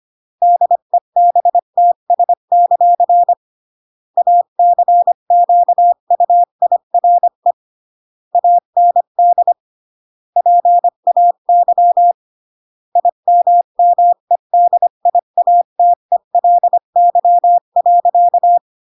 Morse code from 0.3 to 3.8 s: DEBTS;